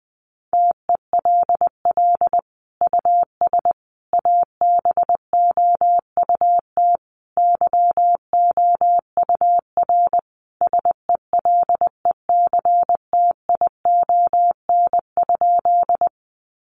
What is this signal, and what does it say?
Morse code, heard as a keyed tone, about 20 wpm, 700 Hz. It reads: TELL US ABOUT YOUR SELECTION?
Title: Sentences from Top 1400 Words 20wpm